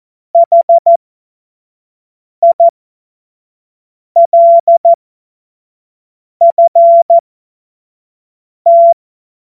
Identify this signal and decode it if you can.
Morse code, heard as a keyed tone, about 14 wpm, spaced as for 5 wpm, 680 Hz. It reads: HILFT
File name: Morsekurs / Witze / Witz 07. September 2021, 25 F70